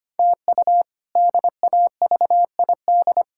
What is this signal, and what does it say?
Morse code, heard as a keyed tone, about 25 wpm, 700 Hz. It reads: TU DAVID
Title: K1USN SST QSO Practice - 25wpm